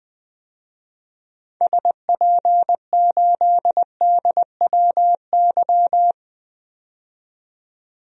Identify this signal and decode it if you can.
Morse code, heard as a keyed tone, about 20 wpm, 700 Hz. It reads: SP8DWY
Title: DXCC-100 Call Signs - 6 Characters - 20wpm